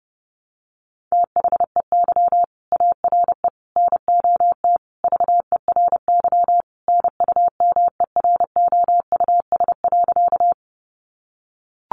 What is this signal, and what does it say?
Morse code, heard as a keyed tone, about 28 wpm, 700 Hz. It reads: THEY ARE NOT VERY NUMEROUS.
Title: Sentences from Top 1400 Words 30wpm